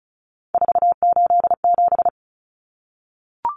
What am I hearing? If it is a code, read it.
Morse code, heard as a keyed tone, about 34 wpm, 700 Hz. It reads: 487
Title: Numbers - 3-Digit 35wpm